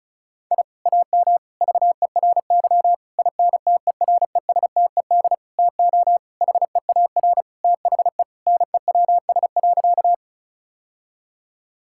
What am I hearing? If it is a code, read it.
Morse code, heard as a keyed tone, about 34 wpm, 700 Hz. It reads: I AM VERY INTERESTED TO HEAR THE NEWS.
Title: Sentences from Top 1000 Words - Review Part 6 - 35wpm